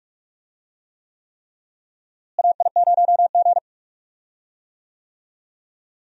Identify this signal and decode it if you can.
Morse code, heard as a keyed tone, about 41 wpm, 700 Hz. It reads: AI0G
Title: USA Call Signs - 4 Characters - Repeated 3x - 45wpm